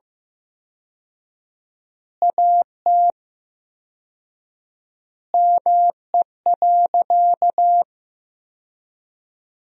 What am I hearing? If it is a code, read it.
Morse code, heard as a keyed tone, about 15 wpm, 700 Hz. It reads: AT ME.
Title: 4x Word-Spacing - Sentences from Top 200 Words - 15wpm